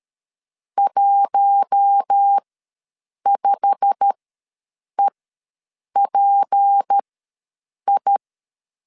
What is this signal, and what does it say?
Morse code, heard as a keyed tone, about 13 wpm, spaced as for 7 wpm, 800 Hz. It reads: 15EPI
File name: LEZIONI / 8 YJ45 / 8 YJ45 ws 12wpm 3146